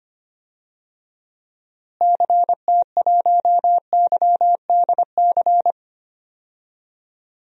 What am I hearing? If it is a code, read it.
Morse code, heard as a keyed tone, about 25 wpm, 700 Hz. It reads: CT1YDC
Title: DXCC-100 Call Signs - 6 Characters - Repeated 3x - 25wpm